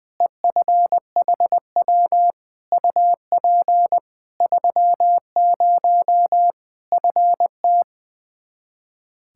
Morse code, heard as a keyed tone, about 21 wpm, 700 Hz. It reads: EFHW UP 30 FT
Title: K1USN SST QSO Practice 20wpm